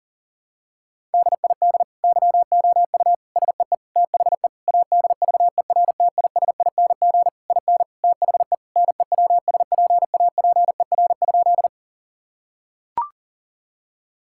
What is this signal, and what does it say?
Morse code, heard as a keyed tone, about 40 wpm, 700 Hz. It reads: DID YOU SEE THE ADVERTISING IN THE NEWSPAPER?
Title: Sentences from Top 1000 Words - Review Part 6 - 40wpm